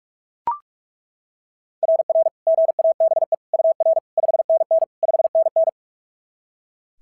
Morse code, heard as a keyed tone, about 39 wpm, 660 Hz. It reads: RR GABE UR 5NN 5NN
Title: SOTA QSO Practice 45wpm